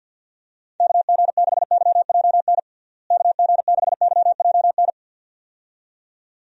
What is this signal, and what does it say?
Morse code, heard as a keyed tone, about 48 wpm, 700 Hz. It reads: KG6YJN KG6YJN
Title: USA Call Signs - 6 Characters - Repeated 2x - 50wpm